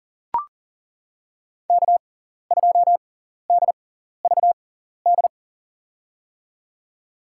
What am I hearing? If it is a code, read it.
Morse code, heard as a keyed tone, about 40 wpm, spaced as for 14 wpm, 700 Hz. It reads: K2DVD
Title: USA Call Signs - 5 Characters - 6x Character Spacing - 40wpm